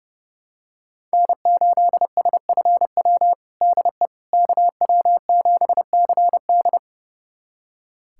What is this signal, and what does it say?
Morse code, heard as a keyed tone, about 32 wpm, 700 Hz. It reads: N8SFW DE KW7CD
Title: USA Callsign DE Callsign - 5 Characters - 30wpm